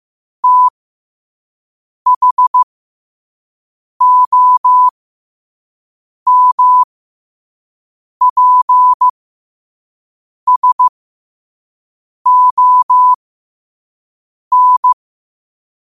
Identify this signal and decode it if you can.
Morse code, heard as a keyed tone, about 15 wpm, spaced as for 5 wpm, 1000 Hz. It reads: THOMPSON
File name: qotdfiles / 5wpm